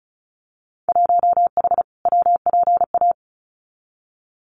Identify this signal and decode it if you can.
Morse code, heard as a keyed tone, about 34 wpm, 700 Hz. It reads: 1H WPA
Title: Winter Field Day Exchanges 35wpm